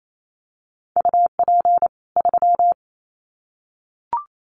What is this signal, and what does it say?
Morse code, heard as a keyed tone, about 28 wpm, 700 Hz. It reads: UP 3
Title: Common QSO Phrases 28wpm